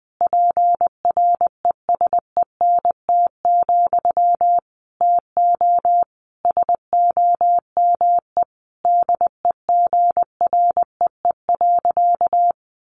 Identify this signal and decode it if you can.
Morse code, heard as a keyed tone, about 20 wpm, 700 Hz. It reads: PRESENT, TO SOME DEGREE.
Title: Sentences from Top 1500 Words 20wpm